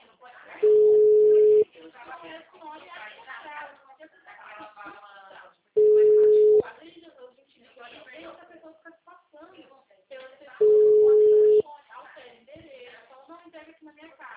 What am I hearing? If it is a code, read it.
Ringback tone (Europe, 425 Hz), with speech.